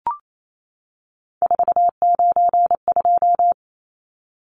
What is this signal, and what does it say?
Morse code, heard as a keyed tone, about 28 wpm, 700 Hz. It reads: 492
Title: Numbers - 3-Digit 28wpm